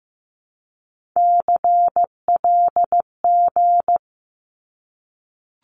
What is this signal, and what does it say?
Morse code, heard as a keyed tone, about 15 wpm, 700 Hz. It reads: CLG
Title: Abbreviations 15wpm